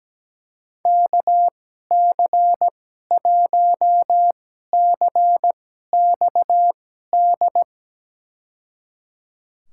Morse code, heard as a keyed tone, about 17 wpm, spaced as for 12 wpm, 700 Hz. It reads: KC1CXD
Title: USA Call Signs - 6 Characters - 2x Character Spacing - 17wpm